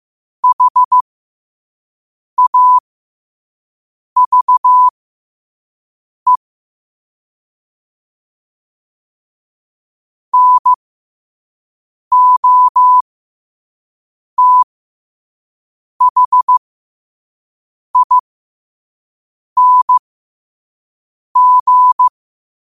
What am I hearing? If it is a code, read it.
Morse code, heard as a keyed tone, about 15 wpm, spaced as for 5 wpm, 1000 Hz. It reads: HAVE NOTHING
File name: qotdfiles / 5wpm